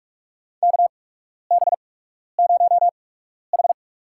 Morse code, heard as a keyed tone, about 46 wpm, spaced as for 12 wpm, 700 Hz. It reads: KB0H
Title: USA Call Signs - 4 Characters - 8x Character Spacing - 45wpm